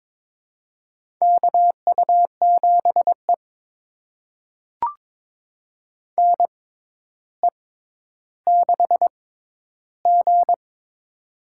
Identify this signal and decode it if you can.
Morse code, heard as a keyed tone, about 22 wpm, spaced as for 11 wpm, 700 Hz. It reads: KU7E NE6G
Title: USA Call Signs - 4 Characters - 6x Character Spacing - 22wpm